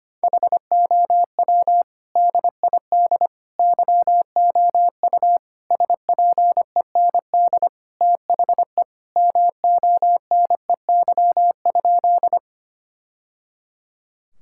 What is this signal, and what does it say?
Morse code, heard as a keyed tone, about 25 wpm, 700 Hz. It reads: HOW DID YOU SPEND THE MONEY?
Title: Sentences from Top 1000 Words - Review Part 5 - 25wpm